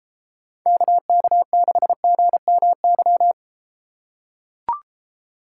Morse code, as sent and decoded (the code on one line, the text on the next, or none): -.- -.- -.... --. -- -.--
KK6GMY